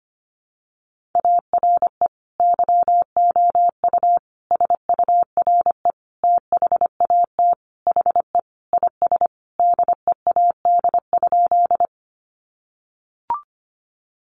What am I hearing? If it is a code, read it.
Morse code, heard as a keyed tone, about 25 wpm, 700 Hz. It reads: ARE YOU SURE THAT HE IS DEAD?
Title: Sentences from Top 1500 Words 25wpm